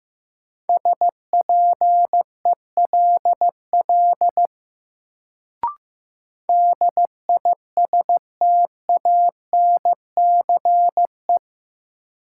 Morse code, heard as a keyed tone, about 15 wpm, 700 Hz. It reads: SPELL DISTANCE
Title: Sentences from Top 1200 Words - New Word Drill Alt - 15wpm